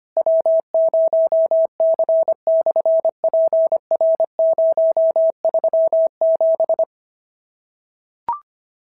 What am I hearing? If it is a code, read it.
Morse code, heard as a keyed tone, about 24 wpm, 650 Hz. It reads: W0C/PR037
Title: SOTA QSO Practice 25wpm